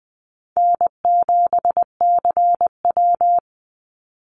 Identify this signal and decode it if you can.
Morse code, heard as a keyed tone, about 20 wpm, 700 Hz. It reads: N7CW
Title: NA SOTA Activator Callsigns 20wpm